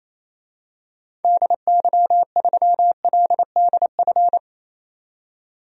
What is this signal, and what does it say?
Morse code, heard as a keyed tone, about 27 wpm, 700 Hz. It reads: DY3LDF
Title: DXCC-100 Call Signs - 6 Characters - 28wpm